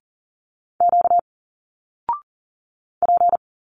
Morse code, heard as a keyed tone, about 40 wpm, 700 Hz. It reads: Q P
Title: Single Letters 40wpm